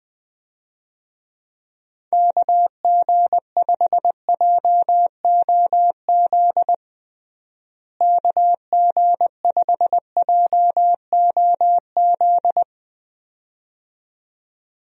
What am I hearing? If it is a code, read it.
Morse code, heard as a keyed tone, about 20 wpm, 700 Hz. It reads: KG5JOZ KG5JOZ
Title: USA Call Signs - 6 Characters - Repeated 2x - 20wpm